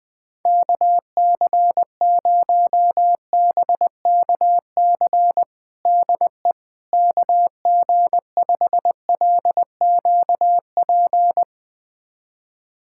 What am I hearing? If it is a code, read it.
Morse code, heard as a keyed tone, about 20 wpm, 700 Hz. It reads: KC0BKC DE KG5LQP
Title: USA Callsign DE Callsign - 6 Characters - 20wpm